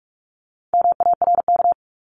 Morse code, heard as a keyed tone, about 44 wpm, 700 Hz. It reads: MARK
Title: Top 500 Words Repeated 45wpm